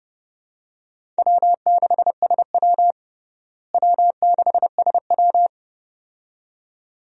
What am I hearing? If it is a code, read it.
Morse code, heard as a keyed tone, about 30 wpm, 700 Hz. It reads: W6SW W6SW
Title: USA Call Signs - 4 Characters - Repeated 2x - 30wpm